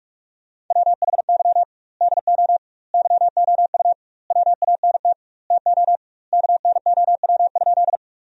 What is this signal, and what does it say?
Morse code, heard as a keyed tone, about 45 wpm, 700 Hz. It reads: WHY DO YOU WANT TO KNOW?
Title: 2x Word-Spacing - Sentences from Top 200 Words - 45wpm